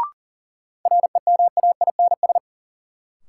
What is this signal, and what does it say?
Morse code, heard as a keyed tone, about 40 wpm, 700 Hz. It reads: REMAINS
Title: Sets of 5 Words Encore - New Word Drill - 40wpm